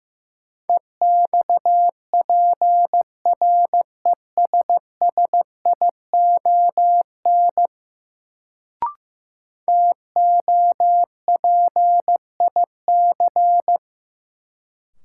Morse code, heard as a keyed tone, about 16 wpm, 700 Hz. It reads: EXPRESSION TOPIC
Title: Sentences from Top 1200 Words - New Word Drill - 15wpm